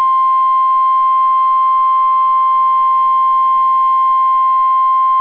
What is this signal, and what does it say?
Emergency Alert System (SAME) tones (the 1050 Hz attention tone). It is heard from 0.4 to 4.7 s.